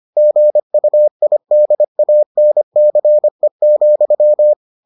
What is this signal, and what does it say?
Morse code, heard as a keyed tone, about 25 wpm, 600 Hz. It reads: GUIDANCE,